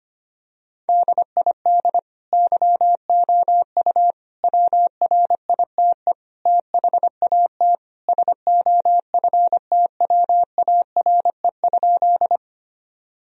Morse code, heard as a keyed tone, about 25 wpm, 700 Hz. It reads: DID YOU WRITE THAT SOFTWARE?
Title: Sentences from Top 1000 Words - Review Part 3 - 25wpm